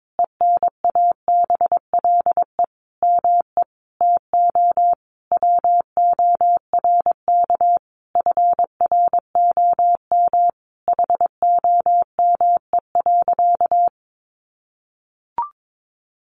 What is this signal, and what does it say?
Morse code, heard as a keyed tone, about 22 wpm, 700 Hz. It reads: ENABLE ME TO WORK FROM HOME.
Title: Sentences from Top 1500 Words 22wpm